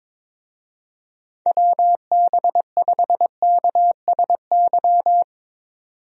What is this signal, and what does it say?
Morse code, heard as a keyed tone, about 22 wpm, 700 Hz. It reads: WB5KSY
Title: USA Call Signs - 6 Characters - 22wpm